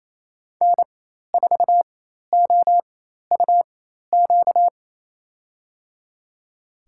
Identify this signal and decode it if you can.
Morse code, heard as a keyed tone, about 28 wpm, spaced as for 13 wpm, 700 Hz. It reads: N4OUQ
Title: USA Call Signs - 4x Character Spacing - 28wpm